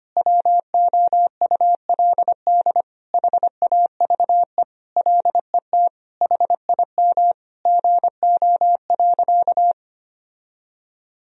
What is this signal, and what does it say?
Morse code, heard as a keyed tone, about 25 wpm, 700 Hz. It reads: WOULD HAVE LET HIM GO.
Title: Sentences from Top 1000 Words - Review Part 1 - 25wpm